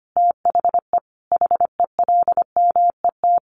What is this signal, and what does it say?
Morse code, heard as a keyed tone, about 25 wpm, 700 Hz. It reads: THE HELMET
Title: A Sound Of Thunder 25wpm